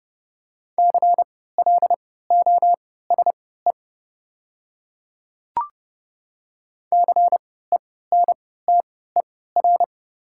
Morse code, heard as a keyed tone, about 30 wpm, spaced as for 17 wpm, 690 Hz. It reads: CLOSE CENTER
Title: 3x Character Spacing - Top 500 Words - 30wpm